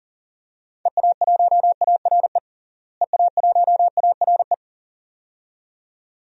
Morse code, heard as a keyed tone, about 40 wpm, 700 Hz. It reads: EA1ARE EA1ARE
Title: DXCC-100 Call Signs - 6 Characters - Repeated 2x - 40wpm